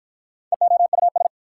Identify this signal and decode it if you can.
Morse code, heard as a keyed tone, about 53 wpm, 690 Hz. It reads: EOUS